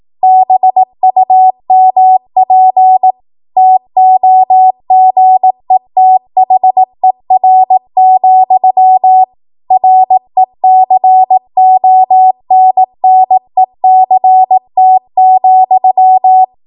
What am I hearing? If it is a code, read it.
Morse code, heard as a keyed tone, about 18 wpm, 750 Hz. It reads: BUMP TOGETHER, RECONNECT,
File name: Morse / 18WPM